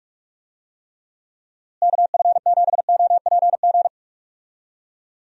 Morse code, heard as a keyed tone, about 41 wpm, 700 Hz. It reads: KF7OPG
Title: USA Call Signs - 6 Characters - Repeated 3x - 45wpm